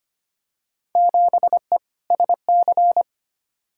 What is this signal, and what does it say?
Morse code, heard as a keyed tone, about 25 wpm, 700 Hz. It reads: 7E SC
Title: Field Day Exchanges Encore - 25wpm